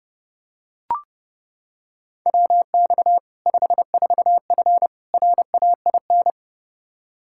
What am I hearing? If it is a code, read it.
Morse code, heard as a keyed tone, about 32 wpm, 700 Hz. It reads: WX 54F RAIN
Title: Practice QSO - Exchange 2 - Info - 30wpm